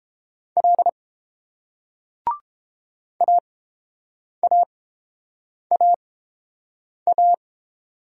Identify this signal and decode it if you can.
Morse code, heard as a keyed tone, about 30 wpm, spaced as for 7 wpm, 700 Hz. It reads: L AAAA